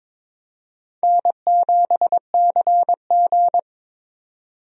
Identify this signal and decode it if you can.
Morse code, heard as a keyed tone, about 22 wpm, 700 Hz. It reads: N7CG
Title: USA Call Signs - 4 Characters - 22wpm